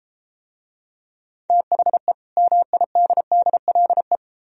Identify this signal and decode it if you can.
Morse code, heard as a keyed tone, about 32 wpm, 700 Hz. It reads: THE MIDDLE